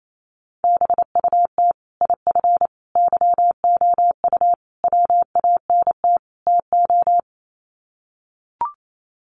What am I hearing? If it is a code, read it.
Morse code, heard as a keyed tone, about 28 wpm, 700 Hz. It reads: BUT IF YOU WANT TO
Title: Sets of 5 Words Encore2 28wpm